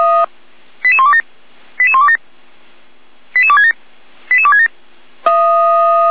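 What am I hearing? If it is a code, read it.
Five-tone selective call (ZVEI), heard as quick runs of five tones, tones of about 70 ms. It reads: 88127 88127 88267 88267
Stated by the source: ZVEI five-tone selective call 88191, received on an SDR.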